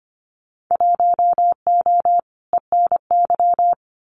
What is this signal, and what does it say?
Morse code, heard as a keyed tone, about 25 wpm, 700 Hz. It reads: 1O ENY
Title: Winter Field Day Exchanges 25wpm